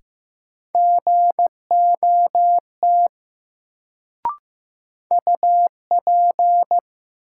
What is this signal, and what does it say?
Morse code, heard as a keyed tone, about 15 wpm, 700 Hz. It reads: GOT UP
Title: Sets of 2 Words - New Word Drill - 15wpm